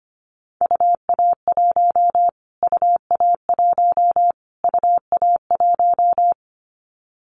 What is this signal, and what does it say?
Morse code, heard as a keyed tone, about 25 wpm, 700 Hz. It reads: UA1 UA1 UA1
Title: Callsign Prefixes for Top 25 DXCC - Repeated 3x - 25wpm